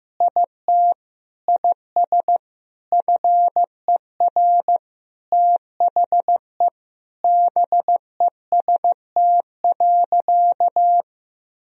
Morse code, heard as a keyed tone, about 16 wpm, 700 Hz. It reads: IT IS FER THE BEST.
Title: Sentences from Top 500 Words and Abbreviations 15wpm